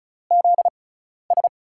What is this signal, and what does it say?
Morse code, heard as a keyed tone, about 34 wpm, spaced as for 12 wpm, 700 Hz. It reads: ZS
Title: USA Call Signs - 5 Characters - 6x Character Spacing - 35wpm